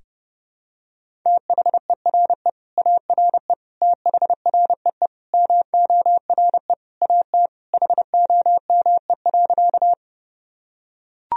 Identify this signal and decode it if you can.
Morse code, heard as a keyed tone, about 30 wpm, 700 Hz. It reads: THERE ARE THREE MORE AT HOME.
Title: Sentences from Top 1000 Words - Review Part 1 - 30wpm